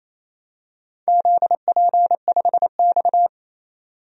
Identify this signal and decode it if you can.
Morse code, heard as a keyed tone, about 28 wpm, 700 Hz. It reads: ZP5X
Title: DXCC-100 Call Signs - 4 Characters - 28wpm